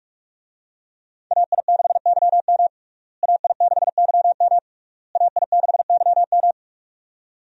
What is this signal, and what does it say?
Morse code, heard as a keyed tone, about 46 wpm, 700 Hz. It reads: AI6YM AI6YM AI6YM
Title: USA Call Signs - 5 Characters - Repeated 3x - 45wpm